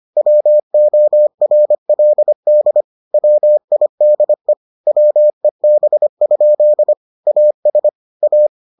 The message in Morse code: WORLD WIDE WEB? AS A

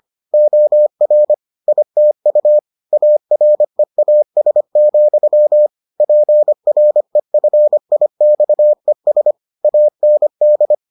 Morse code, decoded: OR ITU AREAS, PREFIXES AND

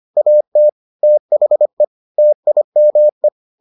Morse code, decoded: AT THE TIME

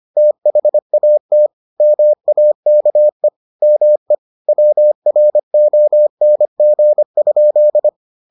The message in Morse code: THAT MAKE ME WRONG?

